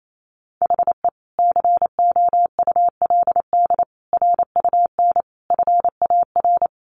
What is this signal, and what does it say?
Morse code, heard as a keyed tone, about 28 wpm, 700 Hz. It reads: HE COULD RUN FAR
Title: Sentences from Top 1100 Words 28wpm